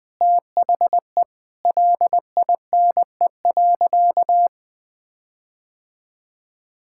Morse code, heard as a keyed tone, about 20 wpm, 700 Hz. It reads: THE LINE.
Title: Sentences with 5-Word Phrases - 20wpm